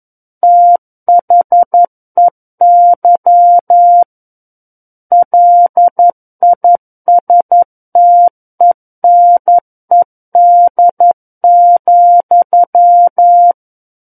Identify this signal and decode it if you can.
Morse code, heard as a keyed tone, about 11 wpm, 700 Hz. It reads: THEY LISTENED,
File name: 11 WPM